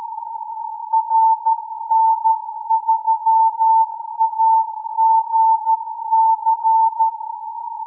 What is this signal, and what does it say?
Morse code, heard as a keyed tone, about 13 wpm, 890 Hz. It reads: RN3AGC
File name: CW AGC with filter 23jan 2015z